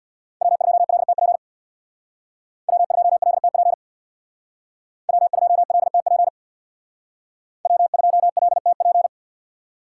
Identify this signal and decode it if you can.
Morse code, heard as a keyed tone, about 64 wpm, 700 Hz. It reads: W2LTP W2LTP W2LTP W2LTP